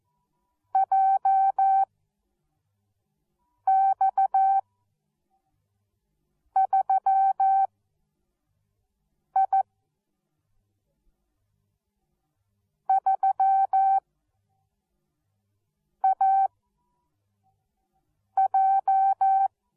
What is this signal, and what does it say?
Morse code, heard as a keyed tone, about 14 wpm, 770 Hz. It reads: J X 3 I 3 A J